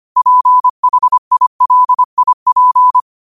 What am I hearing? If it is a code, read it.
Morse code, heard as a keyed tone, about 25 wpm, 1000 Hz. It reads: PHILIP
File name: qotdfiles / 25wpm